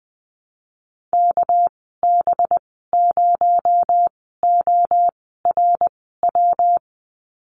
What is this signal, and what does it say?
Morse code, heard as a keyed tone, about 20 wpm, spaced as for 14 wpm, 700 Hz. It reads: KB0ORW
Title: 219 Lesson 35g - Number 0 - Call Signs 20wpm